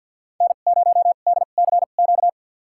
Morse code, heard as a keyed tone, about 50 wpm, spaced as for 36 wpm, 700 Hz. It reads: N0DCQ
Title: USA Call Signs - 5 Characters - 2x Character Spacing - 50wpm